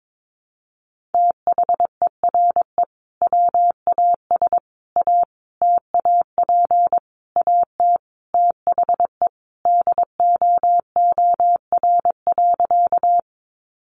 Morse code, heard as a keyed tone, about 22 wpm, 700 Hz. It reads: THERE WAS A TAP AT THE DOOR.
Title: Sentences from Top 1400 Words 22wpm